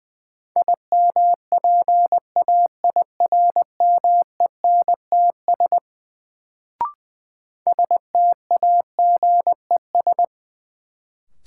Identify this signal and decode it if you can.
Morse code, heard as a keyed tone, about 20 wpm, 700 Hz. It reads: IMPAIRMENTS STAGES
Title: Sets of 5 Words Encore - New Word Drill - 20wpm